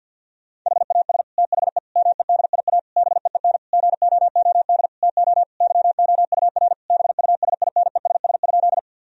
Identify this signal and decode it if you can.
Morse code, heard as a keyed tone, about 52 wpm, 700 Hz. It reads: HAS THE MEDIA BEEN GOOD TO YOUR BUSINESS?
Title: Sentences from Top 1000 Words - Review Part 4 - 50wpm